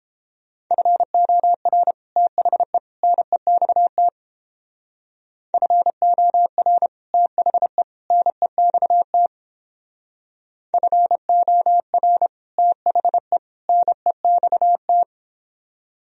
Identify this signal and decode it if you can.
Morse code, heard as a keyed tone, about 31 wpm, 700 Hz. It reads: FOR THE NEXT FOR THE NEXT FOR THE NEXT